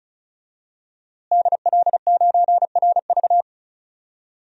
Morse code, heard as a keyed tone, about 36 wpm, 700 Hz. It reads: DL9RV
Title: DXCC-100 Call Signs - 5 Characters - Repeated 2x - 35wpm